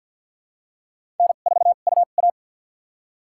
Morse code, heard as a keyed tone, about 53 wpm, spaced as for 37 wpm, 700 Hz. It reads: N4UA
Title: USA Call Signs - 2x Character Spacing - 50wpm